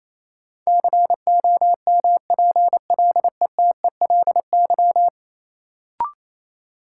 Morse code, heard as a keyed tone, about 28 wpm, 700 Hz. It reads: COMPLETELY
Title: Sentences from Top 1100 Words - New Word Drill - 28wpm